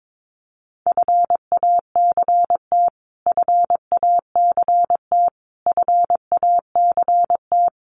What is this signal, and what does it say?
Morse code, heard as a keyed tone, about 22 wpm, 700 Hz. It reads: FACT FACT FACT
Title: Top 500 Words Repeated 22wpm